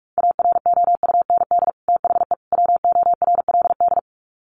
Morse code, heard as a keyed tone, about 44 wpm, 700 Hz. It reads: AROUND THE WORLD
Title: Sets of 3 Words 45wpm